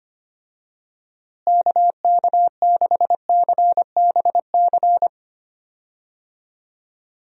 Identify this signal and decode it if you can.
Morse code, heard as a keyed tone, about 25 wpm, 700 Hz. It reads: KK6CBC
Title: USA Call Signs - 6 Characters - Repeated 3x - 25wpm